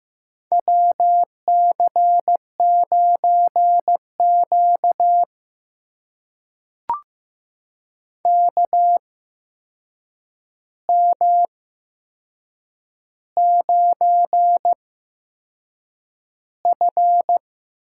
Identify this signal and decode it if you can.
Morse code, heard as a keyed tone, about 15 wpm, spaced as for 6 wpm, 700 Hz. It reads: WC9Q KM9F
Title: USA Call Signs - 4 Characters - 8x Character Spacing - 15wpm